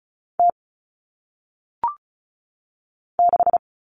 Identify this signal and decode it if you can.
Morse code, heard as a keyed tone, about 36 wpm, 700 Hz. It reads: T 6
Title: All Cut Numbers - 1-Digit 35wpm